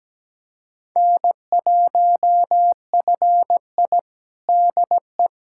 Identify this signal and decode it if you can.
Morse code, heard as a keyed tone, about 17 wpm, 700 Hz. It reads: N1FI DE
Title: USA Callsign DE Callsign - 4 Characters - 17wpm